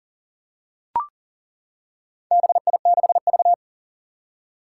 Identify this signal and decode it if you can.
Morse code, heard as a keyed tone, about 40 wpm, 700 Hz. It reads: BI6V
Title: Four Challenging Letter-Number Combinations 40wpm